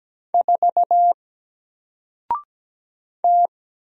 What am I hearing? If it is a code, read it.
Morse code, heard as a keyed tone, about 17 wpm, 700 Hz. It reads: 4 T
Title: Single Letter-Number 17wpm